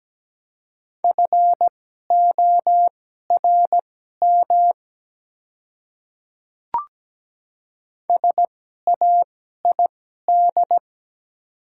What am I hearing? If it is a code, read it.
Morse code, heard as a keyed tone, about 17 wpm, spaced as for 12 wpm, 700 Hz. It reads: FORM SAID
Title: 2x Character Spacing - Top 200 Words - 17wpm